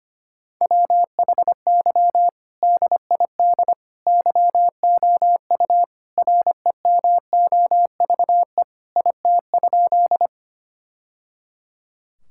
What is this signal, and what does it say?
Morse code, heard as a keyed tone, about 25 wpm, 700 Hz. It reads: WHY DID YOU REMOVE IT?
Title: Sentences from Top 1000 Words - Review Part 5 - 25wpm